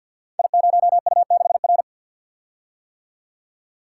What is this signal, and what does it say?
Morse code, heard as a keyed tone, about 48 wpm, 690 Hz. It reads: I0U6R
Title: Code Groups - 5 Letter:Numbers - 50wpm